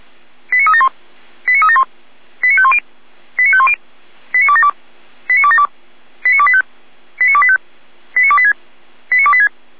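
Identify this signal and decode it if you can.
Five-tone selective call (ZVEI), heard as quick runs of five tones, tones of about 70 ms. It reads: 89371 89371 89510 89510 89272 89272 89285 89285 89286 89286
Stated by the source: ZVEI five-tone selective call received on SDR.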